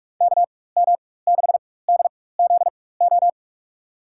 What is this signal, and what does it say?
Morse code, heard as a keyed tone, about 47 wpm, spaced as for 21 wpm, 700 Hz. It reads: KM6DZO
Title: USA Call Signs - 6 Characters - 4x Character Spacing - 45wpm